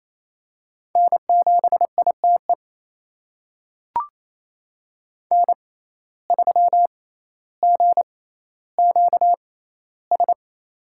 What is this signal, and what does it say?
Morse code, heard as a keyed tone, about 27 wpm, spaced as for 14 wpm, 700 Hz. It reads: N7ITE N3GQS
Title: USA Call Signs - 5 Characters - 6x Character Spacing - 28wpm